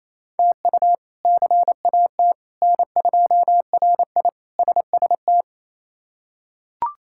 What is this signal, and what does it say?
Morse code, heard as a keyed tone, about 28 wpm, 700 Hz. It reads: TU CAT N2RI SST
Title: K1USN SST QSO Practice - 28wpm